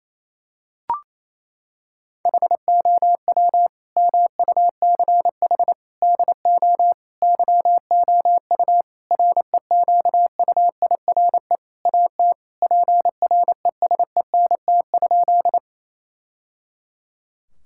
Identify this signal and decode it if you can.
Morse code, heard as a keyed tone, about 28 wpm, 700 Hz. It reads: HOW MUCH DO YOU REQUIRE AT PRESENT?
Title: Sentences from Top 1000 Words - Review Part 6 - 28wpm